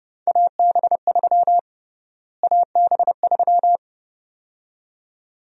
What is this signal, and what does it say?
Morse code, heard as a keyed tone, about 32 wpm, 700 Hz. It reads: AB3 AB3
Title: USA Call Sign Prefixes - Repeated 2x - 30wpm